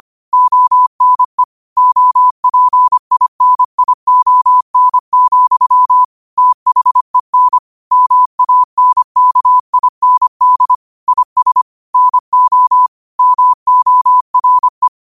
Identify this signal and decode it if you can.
Morse code, heard as a keyed tone, about 25 wpm, 1000 Hz. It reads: ONE OPINION, THEN MANKIND IS NO MORE
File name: qotdfiles / 25wpm